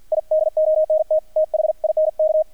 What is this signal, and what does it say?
Morse code, heard as a keyed tone, about 48 wpm, 630 Hz, with radio noise. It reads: I DONT THINK
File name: archived / CRSnet